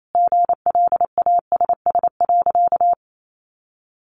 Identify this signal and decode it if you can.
Morse code, heard as a keyed tone, about 28 wpm, 700 Hz. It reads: GLASS.
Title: Sentences from Top 1300 Words 28wpm